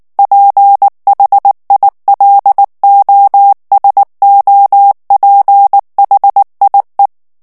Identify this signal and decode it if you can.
Morse code, heard as a keyed tone, about 19 wpm, 800 Hz. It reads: PHILOSOPHIE
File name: CW / FICHIERS-MP3 / FRANCAIS TEXTE CLAIR 2 VITESSE 18 MOTS MINUTES